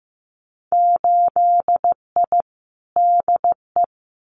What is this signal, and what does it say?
Morse code, heard as a keyed tone, about 15 wpm, 700 Hz. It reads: 8I DE
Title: Winter Field Day Exchanges Encore 15wpm